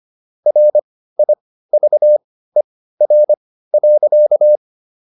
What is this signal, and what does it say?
Morse code, heard as a keyed tone, about 25 wpm, spaced as for 15 wpm, 600 Hz. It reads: RIVER.
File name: SS101 15